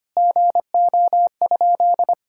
Morse code, heard as a keyed tone, about 25 wpm, 700 Hz. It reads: GO?
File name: practice / 25wpm / Sentences with 2-Word Phrases - 25wpm